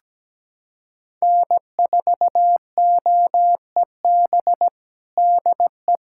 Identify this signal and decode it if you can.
Morse code, heard as a keyed tone, about 17 wpm, 700 Hz. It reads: N4OEB DE
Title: USA Callsign DE Callsign 17wpm